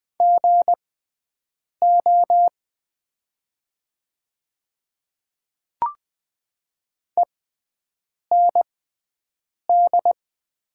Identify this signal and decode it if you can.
Morse code, heard as a keyed tone, about 20 wpm, spaced as for 7 wpm, 700 Hz. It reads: GO END
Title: 6x Character Spacing - Top 200 Words - 20wpm